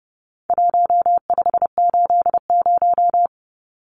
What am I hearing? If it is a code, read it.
Morse code, heard as a keyed tone, about 30 wpm, 700 Hz. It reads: 1580
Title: Numbers - Contest Style - 4-Digit 30wpm